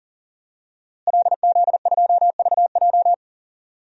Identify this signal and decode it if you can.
Morse code, heard as a keyed tone, about 37 wpm, 700 Hz. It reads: LZ2VJ
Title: DXCC-100 Call Signs - 5 Characters - Repeated 2x - 40wpm